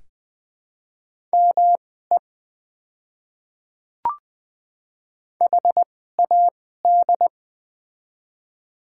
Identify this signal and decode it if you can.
Morse code, heard as a keyed tone, about 20 wpm, spaced as for 15 wpm, 700 Hz. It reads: ME HAD
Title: 2x Character Spacing - Top 200 Words - 20wpm